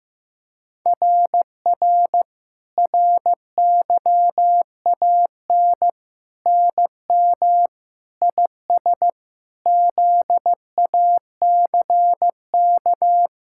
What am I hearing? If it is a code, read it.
Morse code, heard as a keyed tone, about 15 wpm, 700 Hz. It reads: RR RYAN NM IS ZACK
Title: SOTA QSO Practice 15wpm